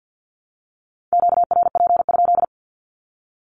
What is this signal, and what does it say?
Morse code, heard as a keyed tone, about 50 wpm, 700 Hz. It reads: QRP?
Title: Q-Codes Repeated 3x 50wpm